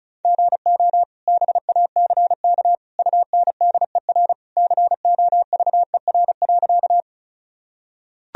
Morse code, heard as a keyed tone, about 34 wpm, 700 Hz. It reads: GO BACK UNDER COVER.
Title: Sentences from Top 1000 Words - Review Part 1 - 35wpm